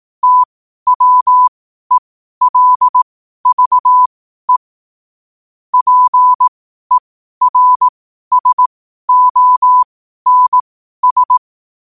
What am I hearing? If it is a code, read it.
Morse code, heard as a keyed tone, about 18 wpm, spaced as for 12 wpm, 1000 Hz. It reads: TWELVE PERSONS
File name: qotdfiles / 12wpm